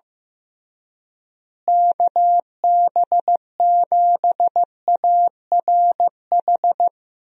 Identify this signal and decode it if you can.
Morse code, heard as a keyed tone, about 15 wpm, 700 Hz. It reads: KB7ARH
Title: USA Call Signs - 6 Characters - Repeated 3x - 15wpm